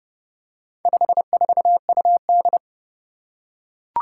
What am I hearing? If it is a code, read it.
Morse code, heard as a keyed tone, about 29 wpm, 700 Hz. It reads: 54UD
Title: Four Challenging Letter-Number Combinations 30wpm